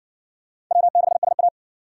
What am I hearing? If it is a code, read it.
Morse code, heard as a keyed tone, about 53 wpm, 700 Hz. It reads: W6SA